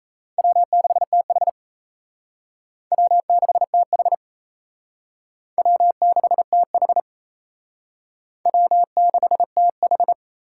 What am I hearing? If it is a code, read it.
Morse code, heard as a keyed tone, about 34 wpm, 700 Hz. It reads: W6TH W6TH W6TH W6TH
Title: Speed Racing - USA 1x2 Call Signs 28wpm